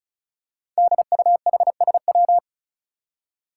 Morse code, heard as a keyed tone, about 36 wpm, 700 Hz. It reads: DUHSW
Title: Five Challenging Letter-Number Combinations 35wpm